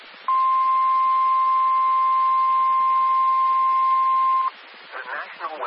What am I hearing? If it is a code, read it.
Emergency Alert System (SAME) tones (the 1050 Hz attention tone), with speech. It is heard from 0.2 to 4.5 s.